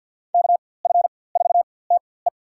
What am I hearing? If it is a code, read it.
Morse code, heard as a keyed tone, about 53 wpm, spaced as for 24 wpm, 700 Hz. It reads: KF4TE
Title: USA Call Signs - 5 Characters - 4x Character Spacing - 50wpm